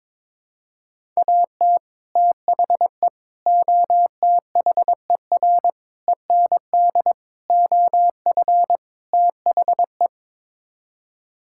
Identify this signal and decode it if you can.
Morse code, heard as a keyed tone, about 22 wpm, 700 Hz. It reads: AT THE OTHER END OF THE
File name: practice / 22wpm / Sets of 6 Words - 22wpm